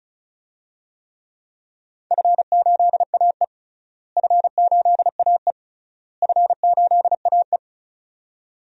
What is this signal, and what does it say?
Morse code, heard as a keyed tone, about 34 wpm, 700 Hz. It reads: F8AE F8AE F8AE
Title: DXCC-100 Call Signs - 4 Characters - Repeated 3x - 35wpm